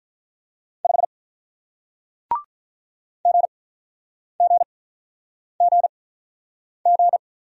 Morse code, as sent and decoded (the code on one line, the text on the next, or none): ..... / --. --. --. --.
5 GGGG